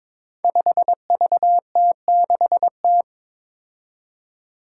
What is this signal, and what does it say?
Morse code, heard as a keyed tone, about 22 wpm, 700 Hz. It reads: 5VT6T
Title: Five Challenging Letter-Number Combinations 22wpm